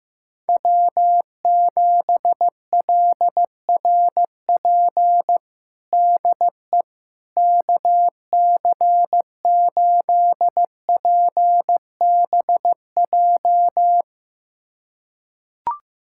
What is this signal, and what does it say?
Morse code, heard as a keyed tone, about 15 wpm, 700 Hz. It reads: W7LRP DE KC8PBJ
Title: USA Callsign DE Callsign 15wpm